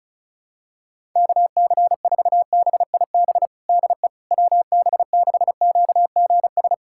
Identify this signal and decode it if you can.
Morse code, heard as a keyed tone, about 36 wpm, 700 Hz. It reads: KC4BIB DE WB6QGS
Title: USA Callsign DE Callsign - 6 Characters - 35wpm